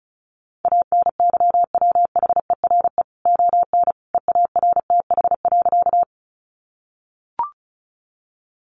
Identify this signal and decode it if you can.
Morse code, heard as a keyed tone, about 34 wpm, 700 Hz. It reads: ANYWHERE ON EARTH.
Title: Sentences from Top 1400 Words 35wpm